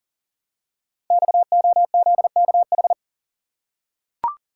Morse code, heard as a keyed tone, about 38 wpm, 700 Hz. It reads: XOZKH
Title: Code Groups - 5 Letter:Numbers - 40wpm